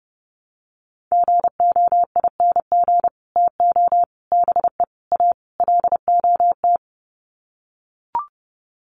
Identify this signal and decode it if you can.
Morse code, heard as a keyed tone, about 30 wpm, 700 Hz. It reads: GOING TO BE A LOT
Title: Sets of 5 Words Encore 30wpm